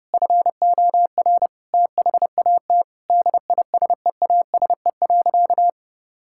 Morse code, heard as a keyed tone, about 30 wpm, 700 Hz. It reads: FOR THAT DISEASE.
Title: Sentences from Top 1000 Words - Review Part 6 - 30wpm